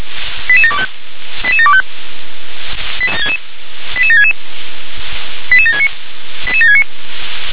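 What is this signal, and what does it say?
Five-tone selective call (ZVEI), heard as quick runs of five tones, tones of about 70 ms. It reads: 88725 88725 88766 88769 88769
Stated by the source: ZVEI five-tone call received by SDR, code 88713.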